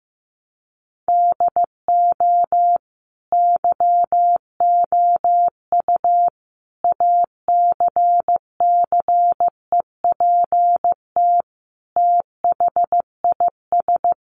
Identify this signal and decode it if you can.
Morse code, heard as a keyed tone, about 15 wpm, 700 Hz. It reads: DO YOU ACCEPT THIS
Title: Sentences from Top 1300 Words 15wpm